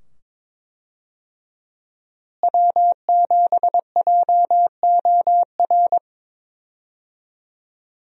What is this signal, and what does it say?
Morse code, heard as a keyed tone, about 22 wpm, 700 Hz. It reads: W7JOR
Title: USA Call Signs - 5 Characters - Repeated 2x - 22wpm